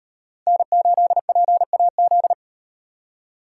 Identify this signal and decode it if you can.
Morse code, heard as a keyed tone, about 36 wpm, 700 Hz. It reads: N8PAZ